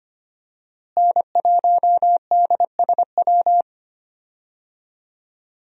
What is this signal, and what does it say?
Morse code, heard as a keyed tone, about 25 wpm, 700 Hz. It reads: N1DSW
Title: USA Call Signs - 5 Characters - Repeated 3x - 25wpm